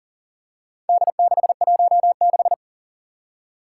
Morse code, heard as a keyed tone, about 39 wpm, 700 Hz. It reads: D616
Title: Four Challenging Letter-Number Combinations 40wpm